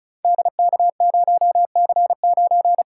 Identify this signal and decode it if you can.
Morse code, heard as a keyed tone, about 36 wpm, 700 Hz. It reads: DK0C9
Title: Code Groups - 5 Letter:Numbers - 35wpm